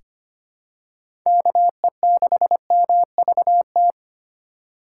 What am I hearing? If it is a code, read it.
Morse code, heard as a keyed tone, about 25 wpm, 700 Hz. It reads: KE6MVT
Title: USA Call Signs - 6 Characters - 4x Character Spacing - 25wpm